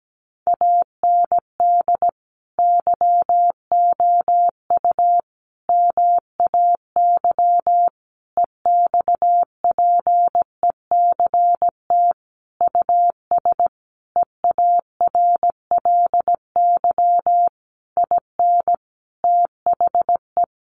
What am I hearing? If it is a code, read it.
Morse code, heard as a keyed tone, about 17 wpm, 700 Hz. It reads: AND YOU MAY EXPECT US EARLY IN THE
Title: Sentences from Top 1500 Words 17wpm